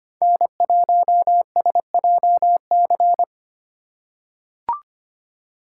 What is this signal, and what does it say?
Morse code, heard as a keyed tone, about 25 wpm, 700 Hz. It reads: N1SJC
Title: USA Call Signs - 2x Character Spacing - 25wpm